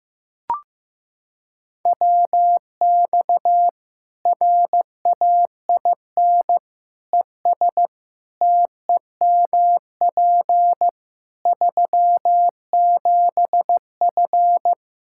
Morse code, heard as a keyed tone, about 15 wpm, 700 Hz. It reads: WX RAIN ES TEMP 37F
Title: Practice QSO - Exchange 2 - Info - 15wpm